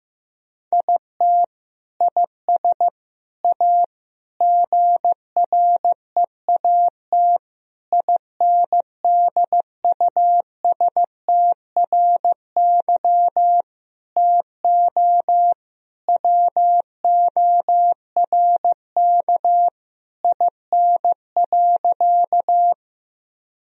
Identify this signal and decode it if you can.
Morse code, heard as a keyed tone, about 15 wpm, 700 Hz. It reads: IT IS A GREAT INDUSTRY TO WORK IN.
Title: Sentences from Top 1000 Words - Review Part 4 - 15wpm